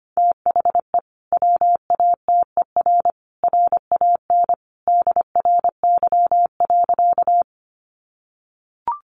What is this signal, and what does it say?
Morse code, heard as a keyed tone, about 25 wpm, 700 Hz. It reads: THE WATER RAN DRY.
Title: Sentences from Top 500 Words 25wpm